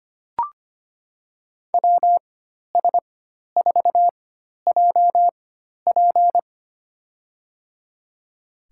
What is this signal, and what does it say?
Morse code, heard as a keyed tone, about 25 wpm, spaced as for 12 wpm, 700 Hz. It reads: WS4JP
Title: USA Call Signs - 5 Characters - 4x Character Spacing - 25wpm